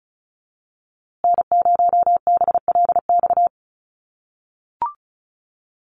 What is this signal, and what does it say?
Morse code, heard as a keyed tone, about 34 wpm, 700 Hz. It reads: N0BLX
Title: USA Call Signs Repeated 2x 35wpm